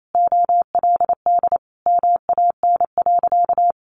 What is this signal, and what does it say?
Morse code, heard as a keyed tone, about 28 wpm, 700 Hz. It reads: OLD MAN.
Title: Sentences from Top 1100 Words 28wpm